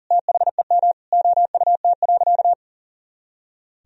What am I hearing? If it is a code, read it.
Morse code, heard as a keyed tone, about 40 wpm, 700 Hz. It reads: THEM OUT.
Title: Sentences from Top 500 Words and Abbreviations 40wpm